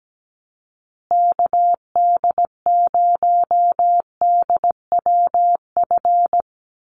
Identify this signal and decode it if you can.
Morse code, heard as a keyed tone, about 17 wpm, 700 Hz. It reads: KD0DWF
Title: USA Call Signs Repeated 2x 17wpm